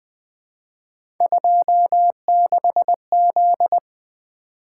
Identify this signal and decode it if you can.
Morse code, heard as a keyed tone, about 20 wpm, 700 Hz. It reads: 26Z
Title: Three Letter-Number Combinations - 20wpm